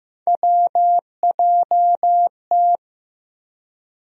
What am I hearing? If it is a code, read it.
Morse code, heard as a keyed tone, about 15 wpm, 700 Hz. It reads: WJT
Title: USA Call Sign Postfixes - 3 Letters 15wpm